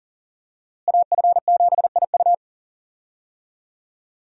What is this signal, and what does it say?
Morse code, heard as a keyed tone, about 37 wpm, 700 Hz. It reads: AF7IU
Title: USA Call Signs - 5 Characters - Repeated 2x - 40wpm